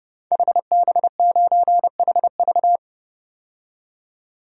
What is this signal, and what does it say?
Morse code, heard as a keyed tone, about 30 wpm, 700 Hz. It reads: HB9HV
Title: DXCC-100 Call Signs - 5 Characters - 30wpm